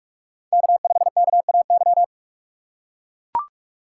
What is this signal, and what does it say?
Morse code, heard as a keyed tone, about 44 wpm, 700 Hz. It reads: K5KAY
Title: USA Call Signs - 5 Characters - 6x Character Spacing - 45wpm